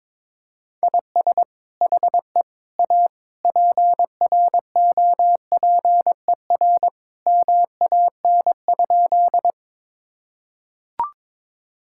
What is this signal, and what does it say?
Morse code, heard as a keyed tone, about 21 wpm, 700 Hz. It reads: IS HE A PROPER MAN?
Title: Sentences from Top 1000 Words - Review Part 5 - 22wpm